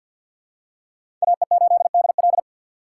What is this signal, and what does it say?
Morse code, heard as a keyed tone, about 53 wpm, 700 Hz. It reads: AE8DL
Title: USA Call Signs - 5 Characters - 6x Character Spacing - 50wpm